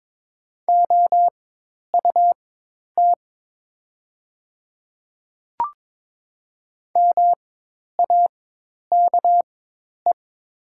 Morse code, heard as a keyed tone, about 22 wpm, spaced as for 10 wpm, 700 Hz. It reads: OUT MAKE
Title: 4x Character Spacing - Top 100 Words - 22wpm